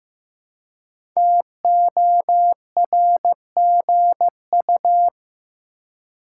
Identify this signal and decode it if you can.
Morse code, heard as a keyed tone, about 15 wpm, 700 Hz. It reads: TORGU
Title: Code Groups - 5 Letters - 15wpm